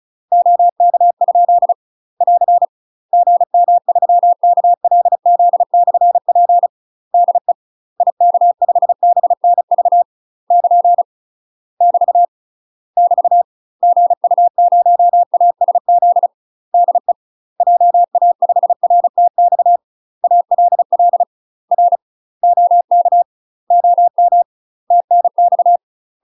Morse code, heard as a keyed tone, about 34 wpm, 700 Hz. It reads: OK ? <AR> GM3KLZ/P DE IK5BNV <KN> <BT> <BT> GU0ASZ DE JA5RTX ALL R OK OM TNX
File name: UK MASTER QSOs TRAINING / 35WPM / UK MASTER QSOs 0004